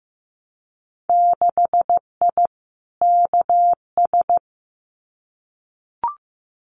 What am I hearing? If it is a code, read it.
Morse code, heard as a keyed tone, about 15 wpm, 700 Hz. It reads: 6I KS
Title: Winter Field Day Exchanges 15wpm